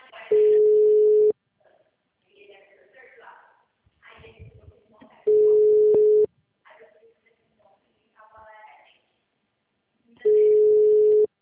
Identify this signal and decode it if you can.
Ringback tone (Europe, 425 Hz), with speech.